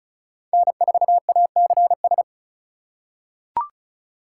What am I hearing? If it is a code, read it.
Morse code, heard as a keyed tone, about 36 wpm, 700 Hz. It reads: N4ACS
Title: USA Call Signs - 5 Characters - 2x Character Spacing - 35wpm